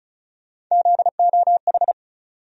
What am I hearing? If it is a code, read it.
Morse code, heard as a keyed tone, about 36 wpm, 700 Hz. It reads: ZOH
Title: USA Call Sign Postfixes - 3 Letters 35wpm